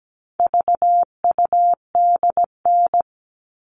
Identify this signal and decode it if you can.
Morse code, heard as a keyed tone, about 17 wpm, 700 Hz. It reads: VUDN
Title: All Cut Numbers - 4-Digit 17wpm